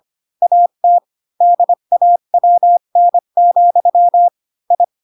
Morse code, heard as a keyed tone, about 25 wpm, 700 Hz. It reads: AT DAWN, I
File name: Pinocchio / Pinocchio ch4 20 25 ik1hge